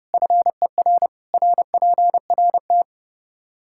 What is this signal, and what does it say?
Morse code, heard as a keyed tone, about 30 wpm, 700 Hz. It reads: FER RPRT
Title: Practice QSO - Exchange 2 - Info - 30wpm